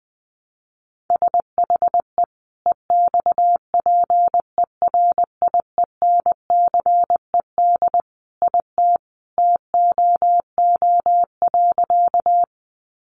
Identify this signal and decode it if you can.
Morse code, heard as a keyed tone, about 20 wpm, 700 Hz. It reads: SHE EXPERIENCED IT TOO.
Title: Sentences from Top 1100 Words 20wpm